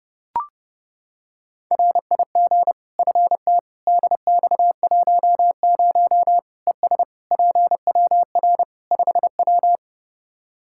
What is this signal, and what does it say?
Morse code, heard as a keyed tone, about 32 wpm, 700 Hz. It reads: RIG FT DX10 ES PWR 5W
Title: Practice QSO - Exchange 2 - Info - 30wpm